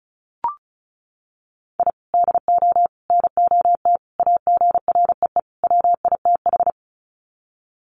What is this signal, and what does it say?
Morse code, heard as a keyed tone, about 34 wpm, 700 Hz. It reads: I DO NOT AGREE WITH
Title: Sets of 5 Words Encore 35wpm